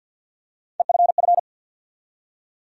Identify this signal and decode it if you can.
Morse code, heard as a keyed tone, about 46 wpm, 700 Hz. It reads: EFF
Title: USA Call Sign Postfixes - 3 Letters 50wpm